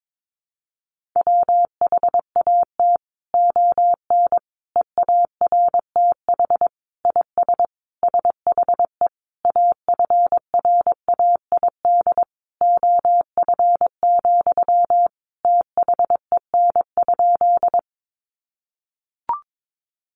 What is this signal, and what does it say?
Morse code, heard as a keyed tone, about 22 wpm, 700 Hz. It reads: WHAT ON EARTH IS SHE AFRAID OF, THEN?
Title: Sentences from Top 1500 Words 22wpm